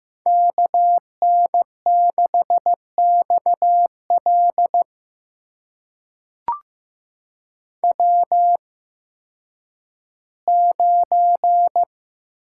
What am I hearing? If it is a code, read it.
Morse code, heard as a keyed tone, about 15 wpm, spaced as for 10 wpm, 700 Hz. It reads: KN6XL W9
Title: USA Call Signs - 5 Characters - 8x Character Spacing - 15wpm